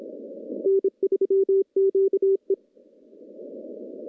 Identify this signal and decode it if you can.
Morse code, heard as a keyed tone, about 30 wpm, 380 Hz. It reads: N3QE
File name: CW-SS-k5tr